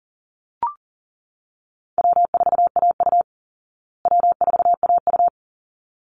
Morse code, heard as a keyed tone, about 40 wpm, 700 Hz. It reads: W4AU W4AU
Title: Easy Call Signs Repeated 2x - 40wpm